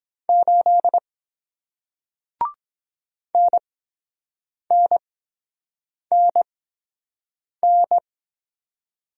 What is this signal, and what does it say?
Morse code, heard as a keyed tone, about 23 wpm, spaced as for 7 wpm, 700 Hz. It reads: 8 NNNN